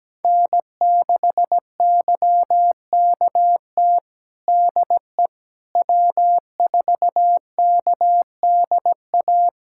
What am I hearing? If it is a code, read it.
Morse code, heard as a keyed tone, about 17 wpm, 700 Hz. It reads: N6YKT DE W4KDA
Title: USA Callsign DE Callsign - 5 Characters - 17wpm